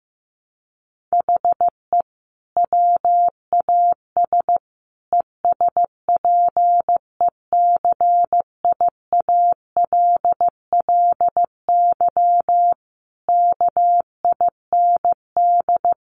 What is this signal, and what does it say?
Morse code, heard as a keyed tone, about 15 wpm, 700 Hz. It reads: HE WAS ESPECIALLY KIND
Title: Sentences from Top 700 Words 15wpm